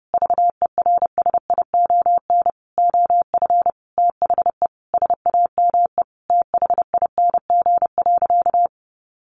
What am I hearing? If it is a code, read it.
Morse code, heard as a keyed tone, about 28 wpm, 700 Hz. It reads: VERSION OF THE SAME THING.
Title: Sentences from Top 1400 Words 30wpm